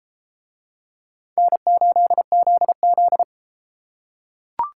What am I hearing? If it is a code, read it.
Morse code, heard as a keyed tone, about 32 wpm, 700 Hz. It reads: N8ZZ